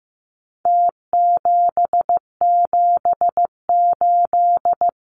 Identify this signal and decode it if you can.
Morse code, heard as a keyed tone, about 15 wpm, 700 Hz. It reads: T778
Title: Numbers - Contest Style - 4-Digit 15wpm